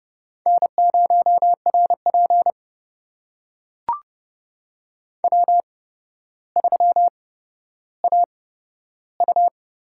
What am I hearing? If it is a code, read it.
Morse code, heard as a keyed tone, about 30 wpm, spaced as for 13 wpm, 700 Hz. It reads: N0RP W3AU
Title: USA Call Signs - 4 Characters - 8x Character Spacing - 30wpm